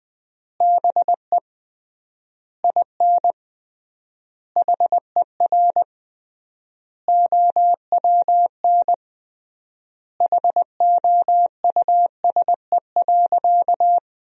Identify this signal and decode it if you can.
Morse code, heard as a keyed tone, about 20 wpm, 700 Hz. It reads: BE IN HER OWN HOUSE.
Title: 3x Word-Spacing - Sentences from Top 300 Words - 20wpm